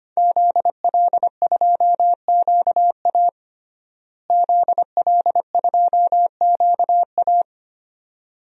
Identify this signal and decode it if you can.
Morse code, heard as a keyed tone, about 25 wpm, 700 Hz. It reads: ZL2QA ZL2QA
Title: DXCC-100 Call Signs - 5 Characters - Repeated 2x - 25wpm